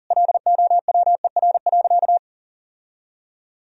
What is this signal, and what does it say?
Morse code, heard as a keyed tone, about 37 wpm, 700 Hz. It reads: LOWER.
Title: Sentences from Top 1000 Words - Review Part 4 - 40wpm